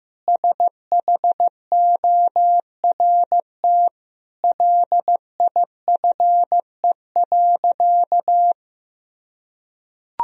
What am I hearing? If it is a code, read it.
Morse code, heard as a keyed tone, about 15 wpm, 700 Hz. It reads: SHORT LIFE.
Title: Sentences from Top 1000 Words - Review Part 2 - 15wpm